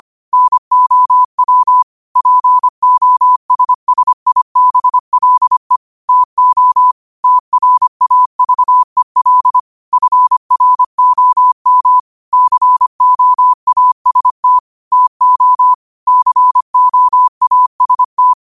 Morse code, heard as a keyed tone, about 25 wpm, 1000 Hz. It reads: NOW POSSIBLE TO TRAVEL FROM COAST TO COAST
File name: qotdfiles / 25wpm